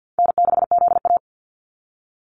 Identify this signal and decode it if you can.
Morse code, heard as a keyed tone, about 50 wpm, 700 Hz. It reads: N6ZA
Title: NA SOTA Activator Callsigns 50wpm